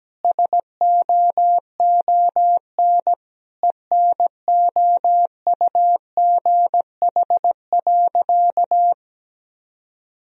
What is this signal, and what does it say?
Morse code, heard as a keyed tone, about 17 wpm, 700 Hz. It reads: SOON ENOUGH.
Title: Sentences from Top 1000 Words - Review Part 3 - 17wpm